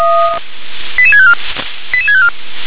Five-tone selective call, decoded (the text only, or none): 88654 88654